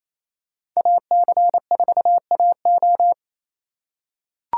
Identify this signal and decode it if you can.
Morse code, heard as a keyed tone, about 28 wpm, 700 Hz. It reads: AC4AO
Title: USA Call Signs - 5 Characters - 2x Character Spacing - 28wpm